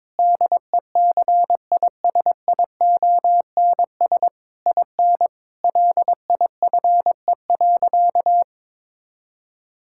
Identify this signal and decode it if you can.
Morse code, heard as a keyed tone, about 22 wpm, 700 Hz. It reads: DECISIONS IN LIFE.
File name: Difficult Sentences with Hints - 22wpm